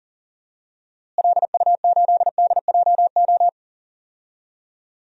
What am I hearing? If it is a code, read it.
Morse code, heard as a keyed tone, about 40 wpm, 700 Hz. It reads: LU8DJO
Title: DXCC-100 Call Signs - 6 Characters - Repeated 3x - 40wpm